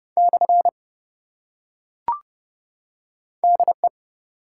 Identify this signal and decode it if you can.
Morse code, heard as a keyed tone, about 30 wpm, 700 Hz. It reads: / DE
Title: Top 180 QSO Elements 30wpm